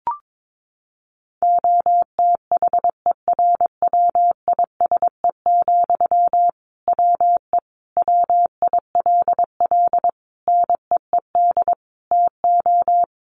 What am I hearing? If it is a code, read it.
Morse code, heard as a keyed tone, about 22 wpm, 700 Hz. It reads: OTHERWISE, WE WILL NEED TO
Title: Sentences from Top 1500 Words 22wpm